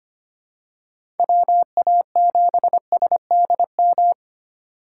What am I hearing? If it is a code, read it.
Morse code, heard as a keyed tone, about 25 wpm, 700 Hz. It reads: WA7SDM
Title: USA Call Signs - 2x Character Spacing - 25wpm